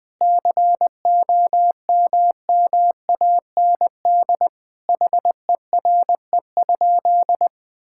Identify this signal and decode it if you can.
Morse code, heard as a keyed tone, about 20 wpm, 700 Hz. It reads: COMMAND HERE?
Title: Sentences from Top 1600 Words 20wpm